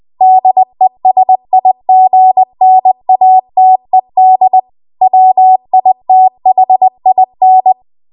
Morse code, heard as a keyed tone, about 20 wpm, 750 Hz. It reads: DESIGNATED WITHIN
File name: Archive / 20 WPM / 20WPM